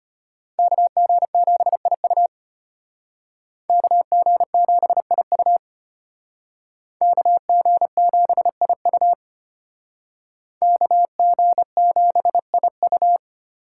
Morse code, heard as a keyed tone, about 32 wpm, 700 Hz. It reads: KG7IU KG7IU KG7IU KG7IU